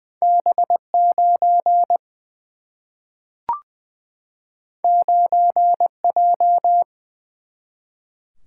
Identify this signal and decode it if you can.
Morse code, heard as a keyed tone, about 20 wpm, 700 Hz. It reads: B 9 9 J
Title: Single-Letter Single-Number 20wpm